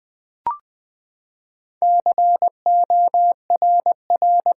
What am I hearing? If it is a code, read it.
Morse code, heard as a keyed tone, about 20 wpm, 700 Hz. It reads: CORR
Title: Build A Word - 4 Letters At A Time - 11-Letter Words - 20wpm